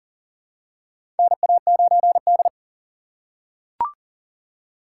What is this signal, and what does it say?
Morse code, heard as a keyed tone, about 40 wpm, 700 Hz. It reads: NA9D
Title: USA Call Signs - 4 Characters - 4x Character Spacing - 40wpm